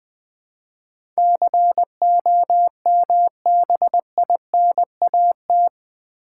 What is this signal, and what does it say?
Morse code, heard as a keyed tone, about 20 wpm, 700 Hz. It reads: COMBINAT
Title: Build A Word - 4 Letters At A Time - 11-Letter Words - 20wpm